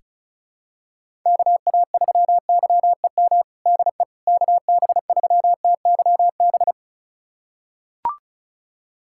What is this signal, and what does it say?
Morse code, heard as a keyed tone, about 36 wpm, 700 Hz. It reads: KA3YEM DE KB3TYB
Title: USA Callsign DE Callsign 35wpm